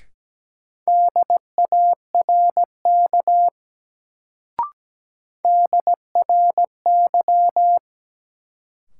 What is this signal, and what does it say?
Morse code, heard as a keyed tone, about 17 wpm, 700 Hz. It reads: DARK DRY
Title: Sentences from Top 500 Words - New Word Drill - 17wpm